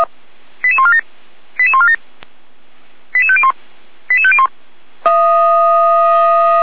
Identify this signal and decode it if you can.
Five-tone selective call (ZVEI), heard as quick runs of five tones, tones of about 80 ms. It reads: 88147 88147 88501 88501